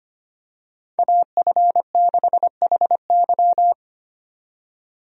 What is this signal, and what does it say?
Morse code, heard as a keyed tone, about 25 wpm, 700 Hz. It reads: AF6HY
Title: USA Call Signs - 5 Characters - 25wpm